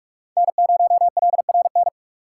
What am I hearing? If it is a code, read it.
Morse code, heard as a keyed tone, about 42 wpm, 700 Hz. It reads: N0LRN
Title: USA Call Signs - 6x Character Spacing - 45wpm